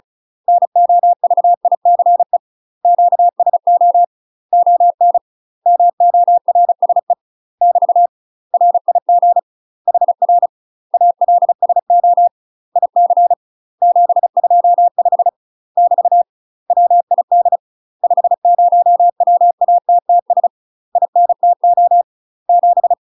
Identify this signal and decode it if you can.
Morse code, heard as a keyed tone, about 34 wpm, 700 Hz. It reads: NOVICE QSO ON MORSE <BT> RIG HR ALSO IC 725 <BT> WID 50WATTS INTO 7
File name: UK MASTER QSOs TRAINING / 35WPM / UK MASTER QSOs 0004